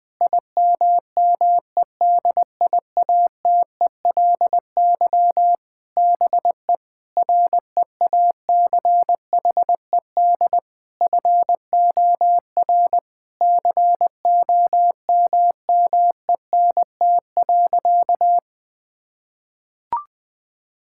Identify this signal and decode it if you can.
Morse code, heard as a keyed tone, about 21 wpm, 700 Hz. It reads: IMMEDIATELY BE REACHED FOR COMMENT.
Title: Sentences with 7-Word Phrases - Spoken Beginning - 20wpm